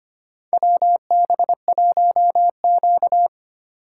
Morse code, heard as a keyed tone, about 25 wpm, 700 Hz. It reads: WB1Q
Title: USA Call Signs - 4 Characters - 3x Character Spacing - 25wpm